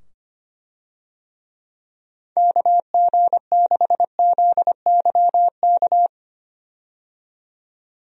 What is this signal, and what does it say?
Morse code, heard as a keyed tone, about 25 wpm, 700 Hz. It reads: KG6ZYK
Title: USA Call Signs - 6 Characters - 25wpm